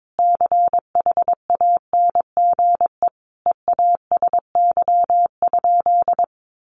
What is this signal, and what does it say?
Morse code, heard as a keyed tone, about 22 wpm, 700 Hz. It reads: CHANGE EASY?
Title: Sentences from Top 1500 Words 22wpm